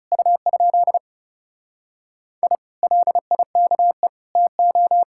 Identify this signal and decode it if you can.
Morse code, heard as a keyed tone, about 40 wpm, 700 Hz. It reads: U? I LIKE TO